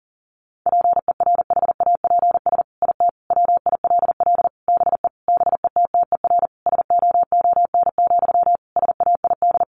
Morse code, heard as a keyed tone, about 40 wpm, 700 Hz. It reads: PERHAPS IT WILL BE BETTER SOON, SAID
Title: Sentences from Top 1500 Words 40wpm